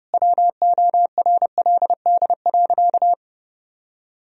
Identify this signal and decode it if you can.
Morse code, heard as a keyed tone, about 30 wpm, 700 Hz. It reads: WORLD.
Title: Sentences from Top 1000 Words - Review Part 2 - 30wpm